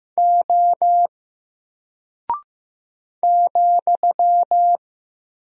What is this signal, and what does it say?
Morse code, heard as a keyed tone, about 15 wpm, 700 Hz. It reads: O ,
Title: Single Character (Letter:Number:Punctuation:Prosign) 15wpm